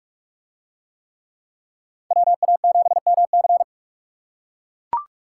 Morse code, heard as a keyed tone, about 44 wpm, 700 Hz. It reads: WA7MC